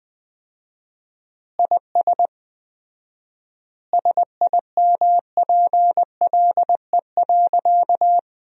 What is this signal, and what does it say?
Morse code, heard as a keyed tone, about 21 wpm, 700 Hz. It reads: IS SIMPLE.
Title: 4x Word-Spacing - Sentences from Top 400 Words - 20wpm